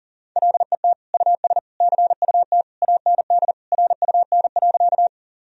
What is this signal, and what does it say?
Morse code, heard as a keyed tone, about 40 wpm, 700 Hz. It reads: LET US CUT AND RUN.
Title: Sentences from Top 1000 Words - Review Part 1 - 40wpm